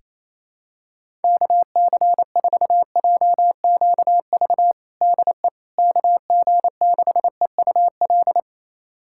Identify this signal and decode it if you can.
Morse code, heard as a keyed tone, about 28 wpm, 700 Hz. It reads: KC4JQV DE KG6EUL
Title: USA Callsign DE Callsign - 6 Characters - 28wpm